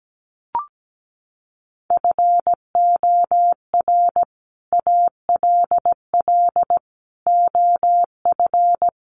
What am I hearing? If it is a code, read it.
Morse code, heard as a keyed tone, about 17 wpm, 700 Hz. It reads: FOR ALL OF
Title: Sets of 4 Words 17wpm